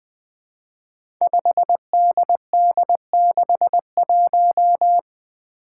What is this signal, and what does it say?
Morse code, heard as a keyed tone, about 20 wpm, 700 Hz. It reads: 5DD61
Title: Five Challenging Letter-Number Combinations 20wpm